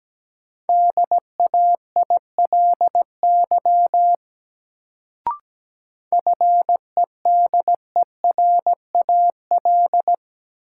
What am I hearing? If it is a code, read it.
Morse code, heard as a keyed tone, about 17 wpm, 700 Hz. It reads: DAILY FEDERAL
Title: Sentences from Top 1100 Words - New Word Drill Alt - 17wpm